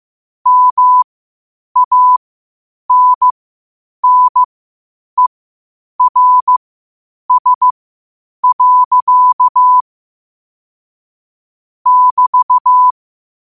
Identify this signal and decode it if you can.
Morse code, heard as a keyed tone, about 15 wpm, spaced as for 8 wpm, 1000 Hz. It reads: MANNERS. <BT>
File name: 8wpm